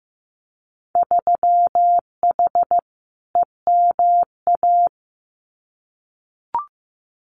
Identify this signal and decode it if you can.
Morse code, heard as a keyed tone, about 15 wpm, 700 Hz. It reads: 3H EMA
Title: Winter Field Day Exchanges Encore 15wpm